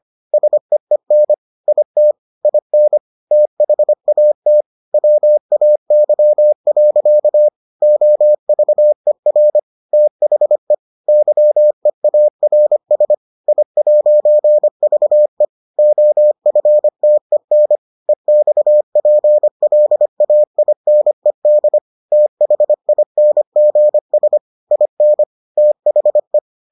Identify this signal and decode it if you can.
Morse code, heard as a keyed tone, about 25 wpm, 600 Hz. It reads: SEEN IT IN THAT WAY. OVER THE YEARS I'VE OFTEN EXPLAINED THINGS IN THE